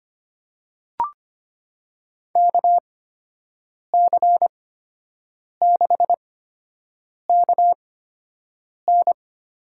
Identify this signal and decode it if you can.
Morse code, heard as a keyed tone, about 25 wpm, spaced as for 7 wpm, 700 Hz. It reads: KC6KN
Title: USA Call Signs - 8x Character Spacing - 25wpm